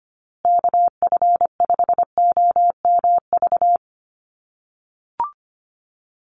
Morse code, heard as a keyed tone, about 25 wpm, 700 Hz. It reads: KF5OMV
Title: USA Call Signs Repeated 2x 25wpm